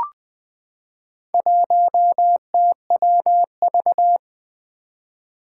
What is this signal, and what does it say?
Morse code, heard as a keyed tone, about 20 wpm, 700 Hz. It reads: 1TWV
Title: Four Challenging Letter-Number Combinations 20wpm